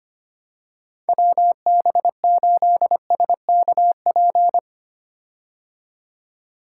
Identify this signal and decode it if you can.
Morse code, heard as a keyed tone, about 25 wpm, 700 Hz. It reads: WB8SKP
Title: K1USN SST QSO Practice - 25wpm